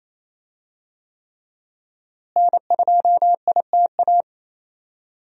Morse code, heard as a keyed tone, about 28 wpm, 700 Hz. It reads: N2ITA
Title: USA Call Signs - 5 Characters - Repeated 3x - 28wpm